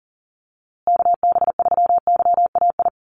Morse code, heard as a keyed tone, about 39 wpm, 700 Hz. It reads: KB3YAI
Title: USA Call Signs Repeated 2x 40wpm